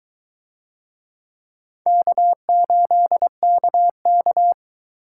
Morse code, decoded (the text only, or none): K8KK